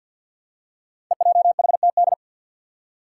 Morse code, heard as a keyed tone, about 51 wpm, 700 Hz. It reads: EJHTD
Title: Five Challenging Letter-Number Combinations 50wpm